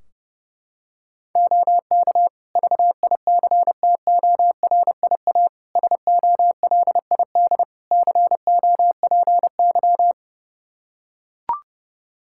Morse code, heard as a keyed tone, about 32 wpm, 700 Hz. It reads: OK VICTORIA SOLID COPY
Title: Practice QSO - Exchange 3 - More Info - 30wpm